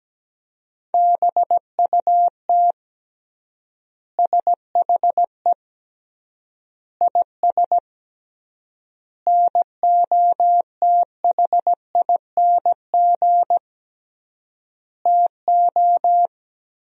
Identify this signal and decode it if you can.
Morse code, heard as a keyed tone, about 17 wpm, 700 Hz. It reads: BUT SHE IS NOTHING TO
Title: 3x Word-Spacing - Sentences from Top 500 Words - 17wpm